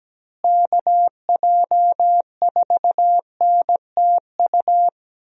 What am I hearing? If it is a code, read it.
Morse code, heard as a keyed tone, about 17 wpm, 700 Hz. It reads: KJ4NTU
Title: USA Call Signs - 6 Characters - Repeated 3x - 17wpm